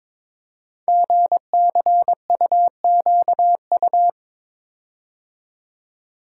Morse code, heard as a keyed tone, about 22 wpm, 700 Hz. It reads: GCUQU
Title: Code Groups - 5 Letters - 22wpm